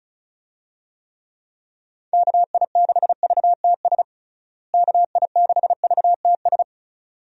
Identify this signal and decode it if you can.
Morse code, heard as a keyed tone, about 34 wpm, 700 Hz. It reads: KI6VTS KI6VTS
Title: USA Call Signs - 6 Characters - Repeated 2x - 35wpm